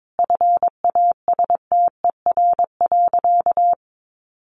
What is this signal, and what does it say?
Morse code, heard as a keyed tone, about 22 wpm, 700 Hz. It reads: FASTER.
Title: Additional Easy Phrases 22wpm